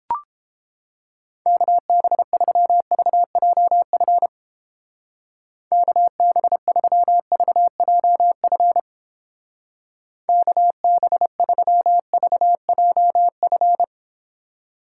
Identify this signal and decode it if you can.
Morse code, heard as a keyed tone, about 29 wpm, 700 Hz. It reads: KB3VJF KB3VJF KB3VJF